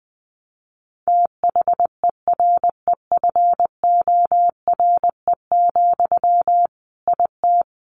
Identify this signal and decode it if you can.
Morse code, heard as a keyed tone, about 19 wpm, 700 Hz. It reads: THEREFORE, IT
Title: Sentences from Top 1500 Words 20wpm